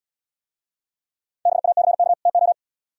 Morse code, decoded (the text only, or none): BACK TO